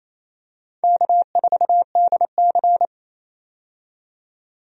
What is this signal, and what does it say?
Morse code, heard as a keyed tone, about 28 wpm, 700 Hz. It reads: K4DC
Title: USA Call Signs - 4 Characters - 28wpm